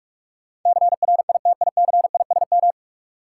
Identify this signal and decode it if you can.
Morse code, heard as a keyed tone, about 46 wpm, 700 Hz. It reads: CRITICISM
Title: Sentences from Top 1300 Words - New Word Drill Alt - 45wpm